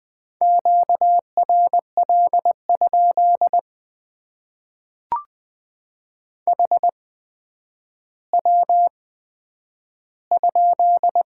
Morse code, heard as a keyed tone, about 20 wpm, spaced as for 10 wpm, 700 Hz. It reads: QRL? HW?
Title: Top 90 QSO Elements - 8x Character Spacing - 20wpm